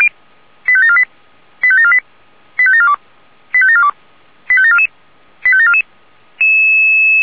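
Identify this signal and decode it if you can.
Five-tone selective call (ZVEI), heard as quick runs of five tones, tones of about 70 ms. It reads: 85748 85748 85742 85742 85744 85744
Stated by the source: ZVEI five-tone selective call 85942 received by SDR.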